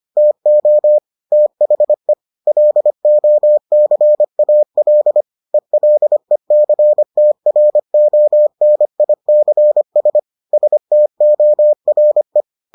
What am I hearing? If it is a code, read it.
Morse code, heard as a keyed tone, about 25 wpm, 600 Hz. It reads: TO THE LOCAL ELECTRONICS STORE